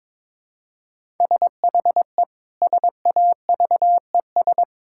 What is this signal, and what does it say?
Morse code, heard as a keyed tone, about 21 wpm, 700 Hz. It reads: SHE SAVES
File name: Difficult Sentences with Hints - 22wpm